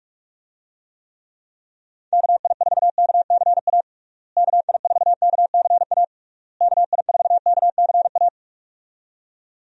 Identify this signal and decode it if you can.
Morse code, heard as a keyed tone, about 47 wpm, 700 Hz. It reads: KI4KCA KI4KCA KI4KCA
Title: USA Call Signs - 6 Characters - Repeated 3x - 45wpm